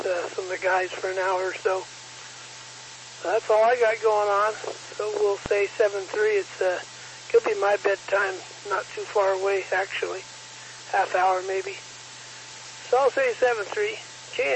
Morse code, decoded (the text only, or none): none